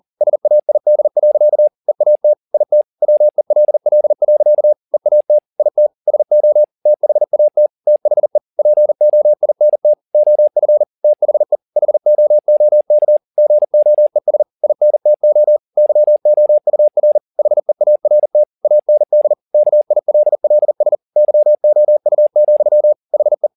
Morse code, heard as a keyed tone, about 40 wpm, 600 Hz. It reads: SAID. EAT IT WELL. EAT IT SO THAT THE POINT OF THE HOOK GOES INTO YOUR HEART AND KILLS YOU, HE